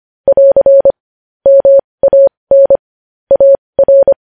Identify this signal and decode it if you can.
Morse code, heard as a keyed tone, about 25 wpm, spaced as for 20 wpm, 550 Hz. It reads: <AR> MAN AR
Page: news-headlines-in-morse-code-at-20-wpm